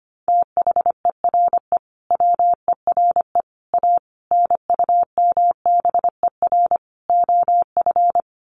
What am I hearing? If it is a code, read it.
Morse code, heard as a keyed tone, about 25 wpm, 700 Hz. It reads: THERE WERE A NUMBER OF
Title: Sets of 5 Words Encore2 25wpm